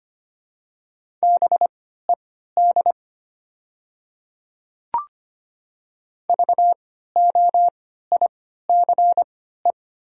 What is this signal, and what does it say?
Morse code, heard as a keyed tone, about 25 wpm, spaced as for 14 wpm, 700 Hz. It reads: BED VOICE
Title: 3x Character Spacing - Top 500 Words - 25wpm